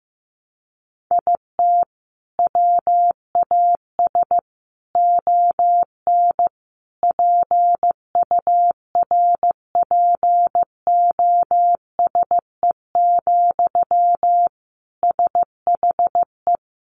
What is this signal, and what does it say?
Morse code, heard as a keyed tone, about 15 wpm, 700 Hz. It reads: IT WAS ON PURPOSE, SHE
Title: Sentences from Top 800 Words 15wpm